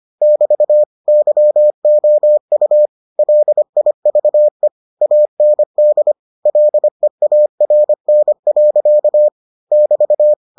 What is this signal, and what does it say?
Morse code, heard as a keyed tone, about 25 wpm, 600 Hz. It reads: <BT> YOU LIVE AND LEARN. <BT>